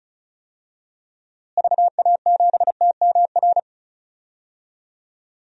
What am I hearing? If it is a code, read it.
Morse code, heard as a keyed tone, about 35 wpm, 700 Hz. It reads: VA7TMR
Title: DXCC-100 Call Signs - 6 Characters - Repeated 3x - 35wpm